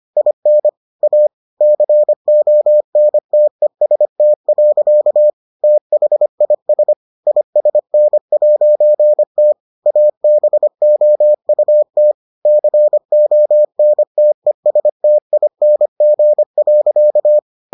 Morse code, heard as a keyed tone, about 25 wpm, 600 Hz. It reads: IN A CONTEST. THIS ISN'T ABOUT CONTESTING.